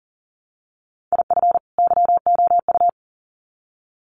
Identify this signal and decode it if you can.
Morse code, heard as a keyed tone, about 40 wpm, 700 Hz. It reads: IF YOU
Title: Sets of 2 Words 40wpm